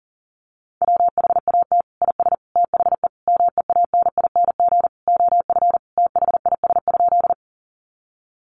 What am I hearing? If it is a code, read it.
Morse code, heard as a keyed tone, about 40 wpm, 700 Hz. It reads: WHAT IS THE MEANING OF THIS?
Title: Sentences from Top 900 Words 40wpm